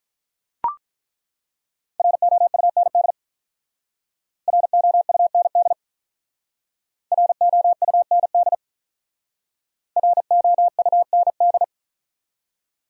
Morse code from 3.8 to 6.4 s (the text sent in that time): ROUND